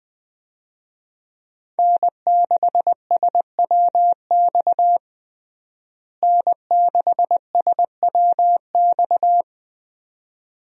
Morse code, heard as a keyed tone, about 21 wpm, 700 Hz. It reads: N6SWX N6SWX
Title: USA Call Signs - 5 Characters - Repeated 2x - 20wpm